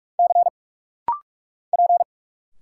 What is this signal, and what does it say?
Morse code, heard as a keyed tone, about 44 wpm, 700 Hz. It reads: C P
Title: Single Letters - Rapid-Fire 45wpm